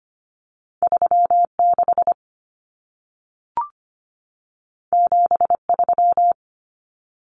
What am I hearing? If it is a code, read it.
Morse code, heard as a keyed tone, about 25 wpm, 700 Hz. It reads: 36 73
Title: Common Cut Numbers - 2-Digit 25wpm